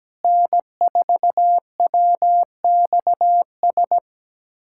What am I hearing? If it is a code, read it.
Morse code, heard as a keyed tone, about 17 wpm, 700 Hz. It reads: N4WXS
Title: USA Callsign DE Callsign - 5 Characters - 17wpm